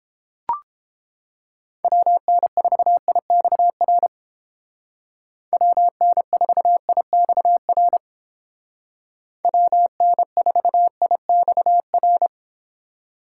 Morse code, heard as a keyed tone, about 29 wpm, 700 Hz. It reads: WN4IXR WN4IXR WN4IXR